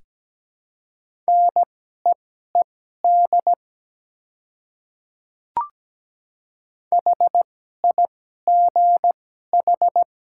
Morse code, heard as a keyed tone, about 17 wpm, spaced as for 12 wpm, 700 Hz. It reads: NEED HIGH
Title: 2x Character Spacing - Top 200 Words - 17wpm